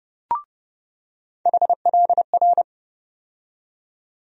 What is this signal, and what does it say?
Morse code, heard as a keyed tone, about 30 wpm, 700 Hz. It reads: HLR
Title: USA Call Sign Postfixes - 3 Letters 30wpm